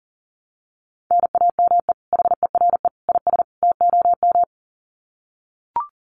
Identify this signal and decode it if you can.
Morse code, heard as a keyed tone, about 40 wpm, 700 Hz. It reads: NAME HERE IS TOM
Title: Common QSO Phrases 40wpm